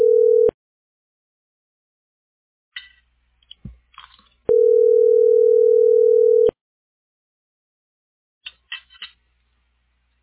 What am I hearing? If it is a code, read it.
Ringback tone (North America), 440 + 480 Hz.